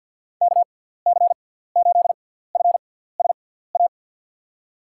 Morse code, heard as a keyed tone, about 47 wpm, spaced as for 17 wpm, 700 Hz. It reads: KC8FSA
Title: USA Call Signs - 6 Characters - 6x Character Spacing - 50wpm